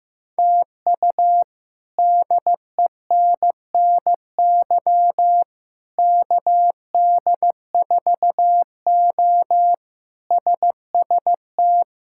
Morse code, heard as a keyed tone, about 15 wpm, 700 Hz. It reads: TU DENNY KD4O SST
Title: K1USN SST QSO Practice - 15wpm